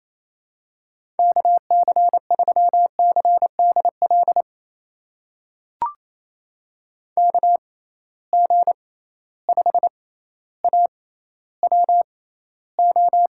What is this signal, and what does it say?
Morse code, heard as a keyed tone, about 28 wpm, spaced as for 14 wpm, 700 Hz. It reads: KC3CDL KG5AWO
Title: USA Call Signs - 6 Characters - 6x Character Spacing - 28wpm